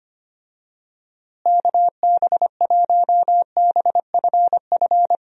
Morse code, heard as a keyed tone, about 25 wpm, 700 Hz. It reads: KB1BFF
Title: USA Call Signs - 6 Characters - Repeated 3x - 25wpm